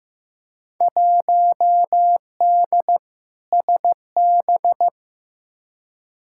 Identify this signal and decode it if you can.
Morse code, heard as a keyed tone, about 15 wpm, 700 Hz. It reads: 1D SB
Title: Field Day Exchanges - 15wpm